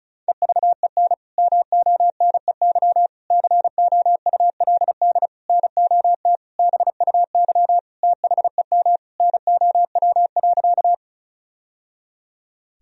Morse code, heard as a keyed tone, about 36 wpm, 700 Hz. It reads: EVEN MONEY COULD NOT BUY THEM NOW.
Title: Sentences from Top 1000 Words - Review Part 4 - 35wpm